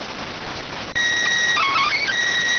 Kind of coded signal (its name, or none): SSTV (slow-scan television)